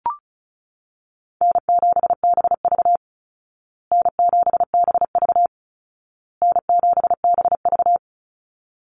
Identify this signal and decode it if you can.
Morse code, heard as a keyed tone, about 34 wpm, 700 Hz. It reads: N7BV N7BV N7BV
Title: Call Signs Repeated 2x - 35wpm